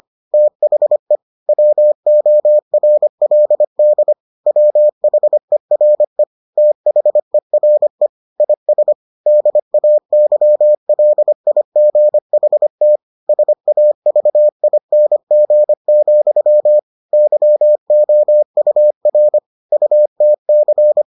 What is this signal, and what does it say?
Morse code, heard as a keyed tone, about 25 wpm, 600 Hz. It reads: THE WORLD WHERE THERE IS DAYLIGHT SAVING, YOUR UTC